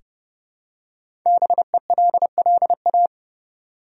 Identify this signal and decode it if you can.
Morse code, heard as a keyed tone, about 30 wpm, 700 Hz. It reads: BELLA
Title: USA Common Names - 30wpm